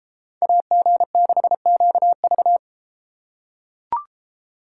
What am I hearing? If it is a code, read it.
Morse code, heard as a keyed tone, about 33 wpm, 700 Hz. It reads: AG6QV